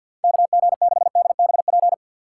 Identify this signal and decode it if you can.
Morse code, heard as a keyed tone, about 54 wpm, 700 Hz. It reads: KG6DBP
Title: USA Call Signs - 6 Characters - 8x Character Spacing - 50wpm